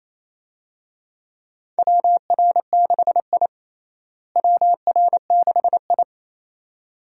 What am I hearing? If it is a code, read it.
Morse code, heard as a keyed tone, about 28 wpm, 700 Hz. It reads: WR6I WR6I
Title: USA Call Signs - 4 Characters - Repeated 2x - 28wpm